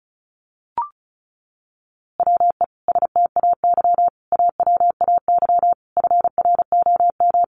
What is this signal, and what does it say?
Morse code, heard as a keyed tone, about 34 wpm, 700 Hz. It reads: WE STAY AWAY FROM
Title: A Sound Of Thunder 35wpm